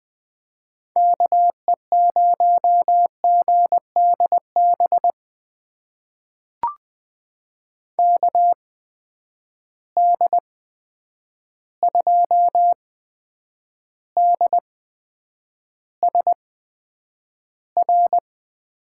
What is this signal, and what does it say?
Morse code, heard as a keyed tone, about 20 wpm, spaced as for 9 wpm, 700 Hz. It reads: KE0GDB KD2DSR
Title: USA Call Signs - 6 Characters - 8x Character Spacing - 20wpm